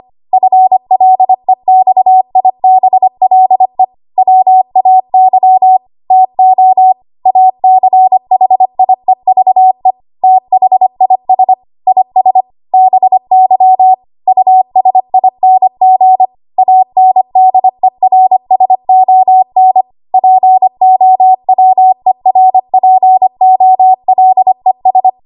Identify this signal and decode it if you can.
Morse code, heard as a keyed tone, about 25 wpm, 750 Hz. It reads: FLEXIBLE WAY TO ACHIEVE THIS IS BY USING ANDERSON POWERPOLES
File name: Morse / 25WPM